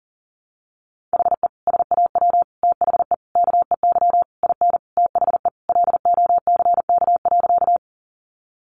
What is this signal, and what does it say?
Morse code, heard as a keyed tone, about 40 wpm, 700 Hz. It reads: HE SAW THE KEY IN THE LOCK.
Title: Sentences from Top 1300 Words 40wpm